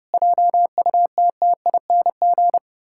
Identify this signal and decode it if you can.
Morse code, heard as a keyed tone, about 30 wpm, 700 Hz. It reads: JUTTING
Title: Pangram Challenge v2 - 30wpm